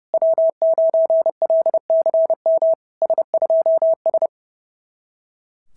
Morse code, heard as a keyed tone, about 31 wpm, 690 Hz. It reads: W9LCM S2S
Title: SOTA QSO Practice 30wpm